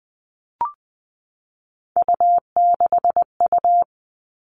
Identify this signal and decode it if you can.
Morse code, heard as a keyed tone, about 20 wpm, 700 Hz. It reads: U6U
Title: All Cut Numbers - 3-Digit 20wpm